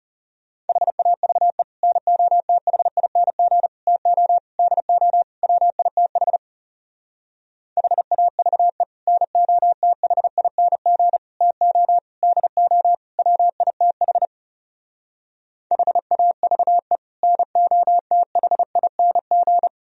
Morse code, heard as a keyed tone, about 40 wpm, 700 Hz. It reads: HAVE NOTHING TO DO WITH HAVE NOTHING TO DO WITH HAVE NOTHING